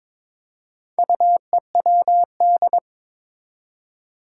Morse code, heard as a keyed tone, about 21 wpm, 700 Hz. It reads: UEWD
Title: Four Challenging Letter-Number Combinations 22wpm